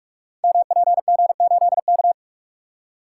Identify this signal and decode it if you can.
Morse code, heard as a keyed tone, about 44 wpm, 700 Hz. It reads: MPG8K
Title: Code Groups - 5 Letter:Numbers - 45wpm